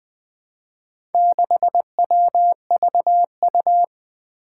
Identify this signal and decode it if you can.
Morse code, heard as a keyed tone, about 20 wpm, 700 Hz. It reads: 6WVU
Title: Four Challenging Letter-Number Combinations 20wpm